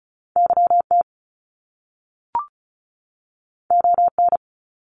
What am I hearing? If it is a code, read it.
Morse code, heard as a keyed tone, about 34 wpm, 700 Hz. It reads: YT ON
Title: Top Two Letter Combinations in Words 35wpm